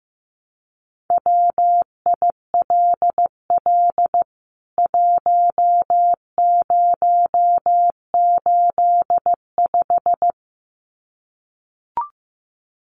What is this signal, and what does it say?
Morse code, heard as a keyed tone, about 15 wpm, 700 Hz. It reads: WILL 1085
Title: CWT Exchange 15wpm